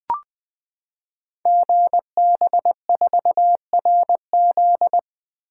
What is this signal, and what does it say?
Morse code, heard as a keyed tone, about 21 wpm, 700 Hz. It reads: GB4RZ
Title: Code Groups - 5 Letter:Numbers - 20wpm